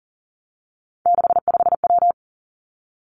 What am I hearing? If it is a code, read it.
Morse code, heard as a keyed tone, about 42 wpm, 700 Hz. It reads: 65W
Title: Three Challenging Letter-Number Combinations 40wpm